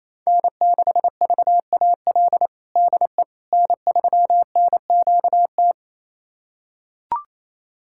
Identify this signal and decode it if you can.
Morse code, heard as a keyed tone, about 28 wpm, 700 Hz. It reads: N6VAL DE N3NQT
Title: USA Callsign DE Callsign - 5 Characters - 28wpm